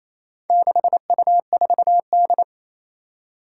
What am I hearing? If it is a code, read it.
Morse code, heard as a keyed tone, about 28 wpm, 700 Hz. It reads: 6U4D
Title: Four Challenging Letter-Number Combinations 28wpm